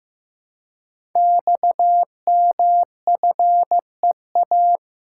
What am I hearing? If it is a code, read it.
Morse code, heard as a keyed tone, about 15 wpm, 700 Hz. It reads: XMFEA
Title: Code Groups - 5 Letter:Numbers - 15wpm